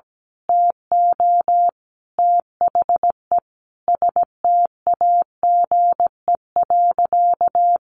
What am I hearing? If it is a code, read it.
Morse code, heard as a keyed tone, about 17 wpm, 700 Hz. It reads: TO THE STAGE.
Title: Sentences from Top 1200 Words 17wpm